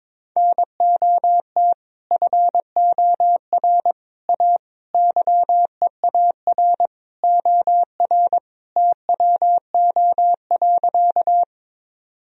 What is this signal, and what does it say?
Morse code, heard as a keyed tone, about 22 wpm, 700 Hz. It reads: NOT FOR A YEAR OR TWO.
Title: Sentences from Top 1000 Words - Review Part 1 - 22wpm